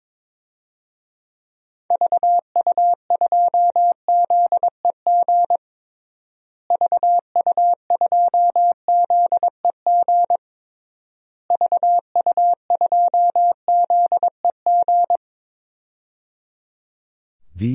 Morse code, heard as a keyed tone, about 22 wpm, 700 Hz. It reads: VU2ZEG VU2ZEG VU2ZEG
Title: DXCC-100 Call Signs - 6 Characters - Repeated 3x - 22wpm